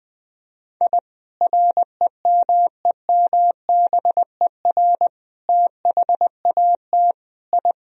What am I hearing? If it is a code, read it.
Morse code, heard as a keyed tone, about 21 wpm, 700 Hz. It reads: I REMEMBER THAT I
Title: Sentences from Top 1000 Words - Review Part 1 - 20wpm